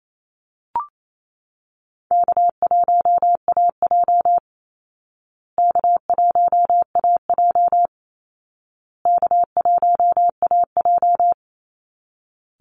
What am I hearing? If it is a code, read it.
Morse code, heard as a keyed tone, about 28 wpm, 700 Hz. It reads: K1AJ K1AJ K1AJ
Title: Easy Call Signs Repeated 3x - 28wpm